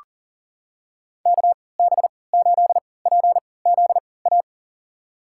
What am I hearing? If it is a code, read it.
Morse code, heard as a keyed tone, about 40 wpm, spaced as for 23 wpm, 700 Hz. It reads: KB8PZA
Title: USA Call Signs - 6 Characters - 3x Character Spacing - 40wpm